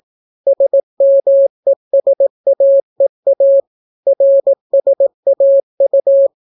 Morse code, heard as a keyed tone, about 18 wpm, 550 Hz. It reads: SMESAEA RSAU